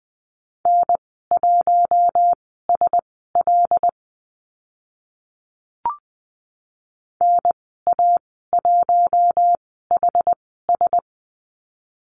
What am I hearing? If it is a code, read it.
Morse code, heard as a keyed tone, about 20 wpm, spaced as for 14 wpm, 700 Hz. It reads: N1SL NA1HS
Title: 076 Lesson 13g - Letter L - Call Signs 20wpm